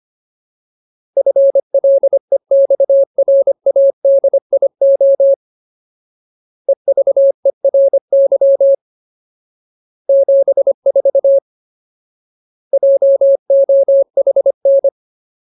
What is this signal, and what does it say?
Morse code, heard as a keyed tone, about 25 wpm, 550 Hz. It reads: FLEXRADIO EVERY 74 JOHN
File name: Ultimate Word List / 25WPM Ultimate